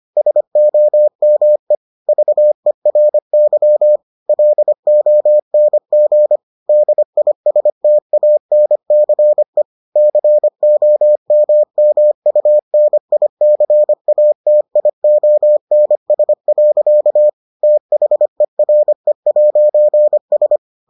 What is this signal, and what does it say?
Morse code, heard as a keyed tone, about 25 wpm, 600 Hz. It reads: SOME VERY LONG DISTANCE COMMUNICATIONS. THERE'S